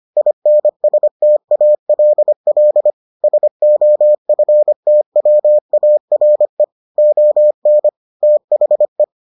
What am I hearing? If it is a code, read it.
Morse code, heard as a keyed tone, about 25 wpm, 600 Hz. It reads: INSTALL SOFTWARE ON THE